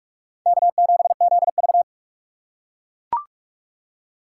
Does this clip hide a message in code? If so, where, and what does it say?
Yes: Morse code from 0.5 to 1.8 s, heard as a keyed tone, about 43 wpm, 700 Hz. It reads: K7ZV